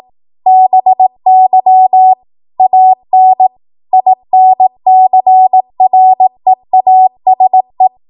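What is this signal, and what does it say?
Morse code, heard as a keyed tone, about 18 wpm, 750 Hz. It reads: BY AN INCREASE
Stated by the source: 18 WPM Morse code practice from a national amateur radio association's code archive.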